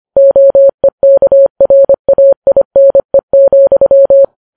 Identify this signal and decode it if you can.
Morse code, heard as a keyed tone, about 25 wpm, 550 Hz. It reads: OEKRAINE,